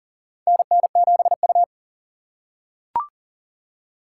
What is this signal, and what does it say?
Morse code, heard as a keyed tone, about 42 wpm, 700 Hz. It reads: NN7U
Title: USA Call Signs - 4 Characters - 6x Character Spacing - 40wpm